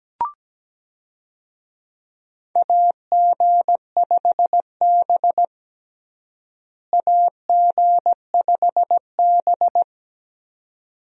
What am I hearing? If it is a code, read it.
Morse code, heard as a keyed tone, about 17 wpm, 700 Hz. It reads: AG5B AG5B
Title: USA Call Signs - 4 Characters - Repeated 2x - 17wpm